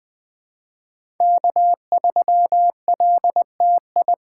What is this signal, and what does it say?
Morse code, heard as a keyed tone, about 20 wpm, 700 Hz. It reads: K3LTI
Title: USA Call Signs - 5 Characters - 4x Character Spacing - 20wpm